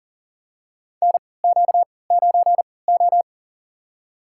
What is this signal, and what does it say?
Morse code, heard as a keyed tone, about 40 wpm, spaced as for 23 wpm, 700 Hz. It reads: NQ9O
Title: USA Call Signs - 4 Characters - 3x Character Spacing - 40wpm